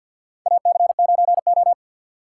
Morse code, heard as a keyed tone, about 47 wpm, 700 Hz. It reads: AC9O
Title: USA Call Signs - 4 Characters - 2x Character Spacing - 50wpm